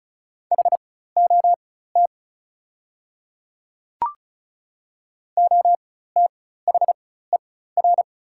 Morse code, heard as a keyed tone, about 35 wpm, spaced as for 16 wpm, 700 Hz. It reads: HOT OTHER
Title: 4x Character Spacing - Top 100 Words - 35wpm